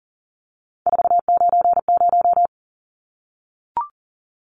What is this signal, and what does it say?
Morse code, heard as a keyed tone, about 40 wpm, 700 Hz. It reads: 490
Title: Numbers - 3-Digit 40wpm